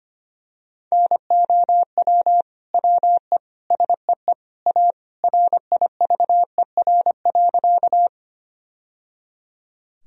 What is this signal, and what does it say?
Morse code, heard as a keyed tone, about 25 wpm, 700 Hz. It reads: NOW WE SEE A RIVER.
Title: Sentences from Top 1000 Words - Review Part 1 - 25wpm